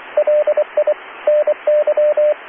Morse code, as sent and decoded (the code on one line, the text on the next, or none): .-.. .. / -. -.--
LI NY